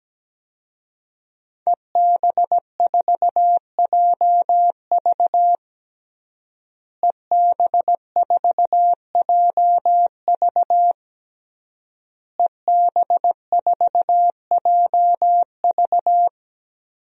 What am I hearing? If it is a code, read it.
Morse code, heard as a keyed tone, about 17 wpm, 700 Hz. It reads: EB4JV EB4JV EB4JV
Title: DXCC-100 Call Signs - 5 Characters - Repeated 3x - 17wpm